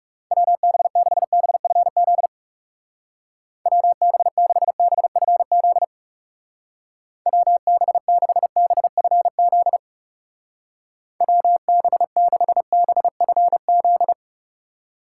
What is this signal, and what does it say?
Morse code, heard as a keyed tone, about 36 wpm, 700 Hz. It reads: WB6BFZ WB6BFZ WB6BFZ WB6BFZ
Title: Speed Racing - USA 2x3 Call Signs 30wpm